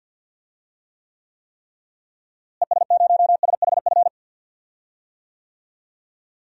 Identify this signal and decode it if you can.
Morse code, heard as a keyed tone, about 52 wpm, 700 Hz. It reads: ES0SHF
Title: DXCC-100 Call Signs - 6 Characters - Repeated 3x - 50wpm